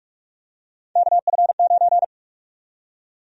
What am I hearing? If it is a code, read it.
Morse code, heard as a keyed tone, about 47 wpm, 700 Hz. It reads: KF9
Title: USA Call Sign Prefixes - Repeated 2x - 45wpm